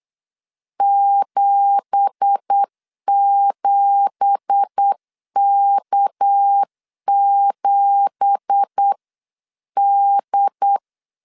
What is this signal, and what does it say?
Morse code, heard as a keyed tone, about 8 wpm, 800 Hz. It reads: 77K7D
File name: 8wpm